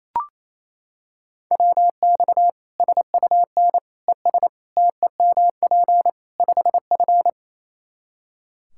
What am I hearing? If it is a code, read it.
Morse code, heard as a keyed tone, about 28 wpm, 700 Hz. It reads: WX SUN ES TEMP 5F
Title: Practice QSO - Exchange 2 - Info - 28wpm